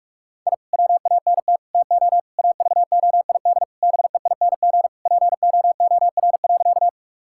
Morse code, heard as a keyed tone, about 44 wpm, 700 Hz. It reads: I WANT TO AVOID BEING POOR.
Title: Sentences from Top 1000 Words - Review Part 6 - 45wpm